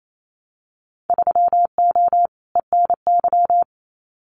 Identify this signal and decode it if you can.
Morse code, heard as a keyed tone, about 28 wpm, 700 Hz. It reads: 3O ENY
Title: Winter Field Day Exchanges Encore 28wpm